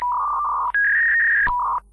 Fax machine handshake.